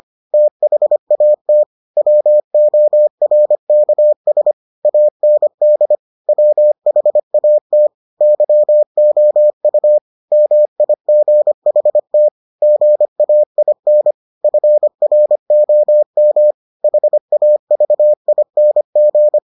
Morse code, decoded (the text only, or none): THAT WORKS AND WHAT YOU MIGHT GAIN FROM HAVING